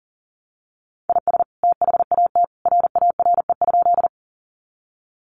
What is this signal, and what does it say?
Morse code, heard as a keyed tone, about 37 wpm, 700 Hz. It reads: IS THAT RARE?
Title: Sentences from Top 1100 Words 40wpm